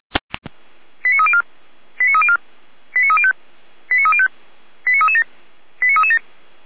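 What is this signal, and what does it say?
Five-tone selective call (ZVEI), heard as quick runs of five tones, tones of about 70 ms. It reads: 89304 89304 89305 89305 89337 89337
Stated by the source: Received by SDR; ZVEI five-tone call.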